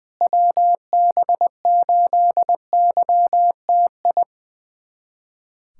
Morse code, heard as a keyed tone, about 20 wpm, 700 Hz. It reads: WB8YTI
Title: USA Callsign DE Callsign - 6 Characters - 20wpm